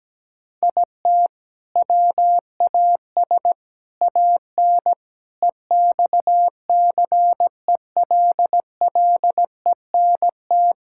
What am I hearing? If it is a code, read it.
Morse code, heard as a keyed tone, about 17 wpm, 700 Hz. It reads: IT WAS AN EXCELLENT
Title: Sentences from Top 1000 Words - Review Part 5 - 17wpm